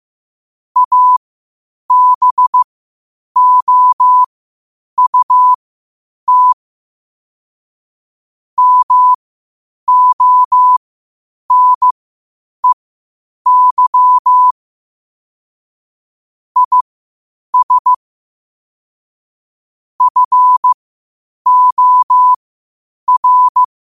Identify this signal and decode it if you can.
Morse code, heard as a keyed tone, about 15 wpm, spaced as for 8 wpm, 1000 Hz. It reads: ABOUT MONEY IS FOR
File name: qotdfiles / 8wpm